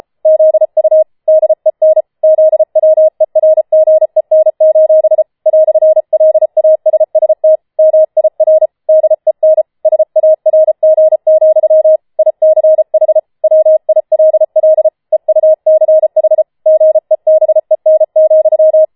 Morse code, heard as a keyed tone, about 33 wpm, 630 Hz. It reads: ZU DEN ZWERGEN: "LASST MIR DEN SARG, ICH WILL EUCH GEBEN,